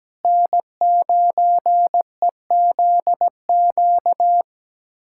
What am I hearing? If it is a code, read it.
Morse code, heard as a keyed tone, about 17 wpm, 700 Hz. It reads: N9EZQ
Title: USA Callsign DE Callsign - 5 Characters - 17wpm